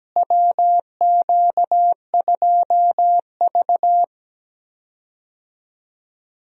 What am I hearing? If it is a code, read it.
Morse code, heard as a keyed tone, about 17 wpm, 700 Hz. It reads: WQ2V
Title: USA Call Signs - 4 Characters - Repeated 2x - 17wpm